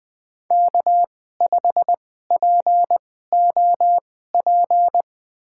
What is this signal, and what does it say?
Morse code, heard as a keyed tone, about 20 wpm, spaced as for 14 wpm, 700 Hz. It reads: K5POP
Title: USA Call Signs - 2x Character Spacing - 20wpm